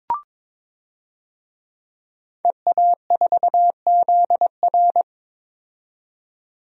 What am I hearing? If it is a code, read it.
Morse code, heard as a keyed tone, about 21 wpm, 700 Hz. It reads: EA4ZR
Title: DXCC-100 Call Signs - 5 Characters - 22wpm